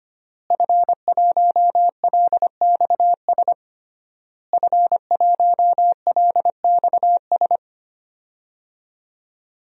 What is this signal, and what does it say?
Morse code, heard as a keyed tone, about 25 wpm, 700 Hz. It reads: F1LXS F1LXS
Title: DXCC-100 Call Signs - 5 Characters - Repeated 3x - 25wpm